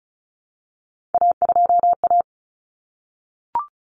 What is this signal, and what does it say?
Morse code, heard as a keyed tone, about 34 wpm, 700 Hz. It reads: A2A
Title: Common Cut Numbers - 3-Digit 35wpm